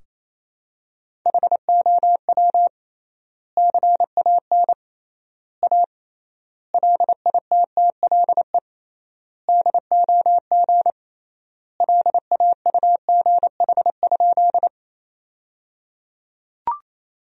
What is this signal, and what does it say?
Morse code, heard as a keyed tone, about 28 wpm, 700 Hz. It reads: HOW CAN A LITTLE DOG LAUGH?
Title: 3x Word-Spacing - Sentences from Top 500 Words - 28wpm